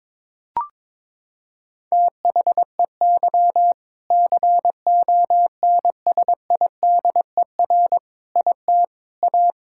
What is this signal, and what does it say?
Morse code, heard as a keyed tone, about 22 wpm, 700 Hz. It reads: THEY CONSIDER IT A
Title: Sentences from Top 1000 Words - Review Part 6 - 22wpm